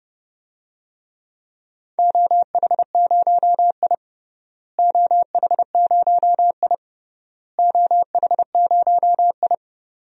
Morse code, heard as a keyed tone, about 30 wpm, 700 Hz. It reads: OH0I OH0I OH0I
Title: DXCC-100 Call Signs - 4 Characters - Repeated 3x - 30wpm